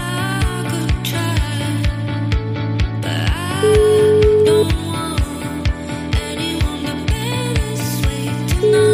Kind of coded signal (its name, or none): telephone call progress tone